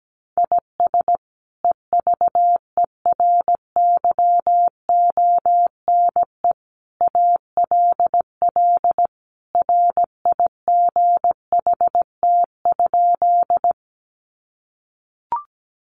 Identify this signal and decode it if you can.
Morse code, heard as a keyed tone, about 17 wpm, 700 Hz. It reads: IS EVERYONE ALL RIGHT?
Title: Sentences from Top 800 Words 17wpm